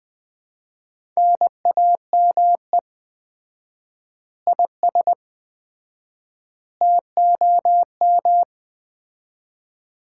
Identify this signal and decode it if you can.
Morse code, heard as a keyed tone, about 20 wpm, 700 Hz. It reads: NAME IS TOM
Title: Common QSO Phrases - 4x Word-Spacing - 20wpm